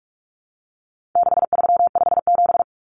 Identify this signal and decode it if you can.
Morse code, heard as a keyed tone, about 44 wpm, 700 Hz. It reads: 6357
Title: Common Cut Numbers - 4-Digit 45wpm